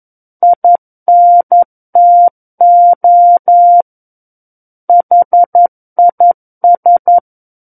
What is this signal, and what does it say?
Morse code, heard as a keyed tone, about 11 wpm, 700 Hz. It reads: INTO HIS
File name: Book One / 02-Chapter 2 - 11 WPM